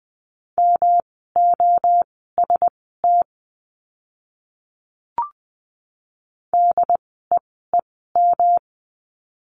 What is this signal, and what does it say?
Morse code, heard as a keyed tone, about 20 wpm, spaced as for 14 wpm, 700 Hz. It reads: MOST DEEM
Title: 108 Lesson 18f - Letter m - Words 20wpm